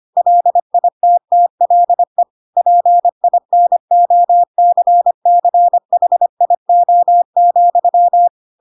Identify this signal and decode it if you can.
Morse code, heard as a keyed tone, about 25 wpm, 700 Hz. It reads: LITTLE PINOCCHIO,
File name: Pinocchio / Pinocchio ch7 25 25 ik1hge